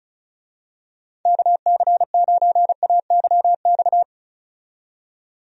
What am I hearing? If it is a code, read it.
Morse code, heard as a keyed tone, about 36 wpm, 700 Hz. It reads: KC9AYX
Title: USA Call Signs - 6 Characters - 35wpm